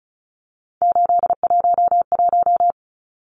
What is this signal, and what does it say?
Morse code, heard as a keyed tone, about 34 wpm, 700 Hz. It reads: 811
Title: Numbers - 3-Digit 35wpm